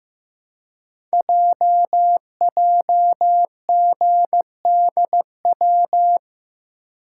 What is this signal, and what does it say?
Morse code, heard as a keyed tone, about 15 wpm, 690 Hz. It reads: JJGDW
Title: Code Groups - 5 Letters - 15wpm